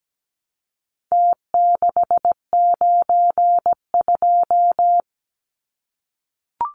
Morse code, heard as a keyed tone, about 17 wpm, 700 Hz. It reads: T692
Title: Numbers - Contest Style - 4-Digit 17wpm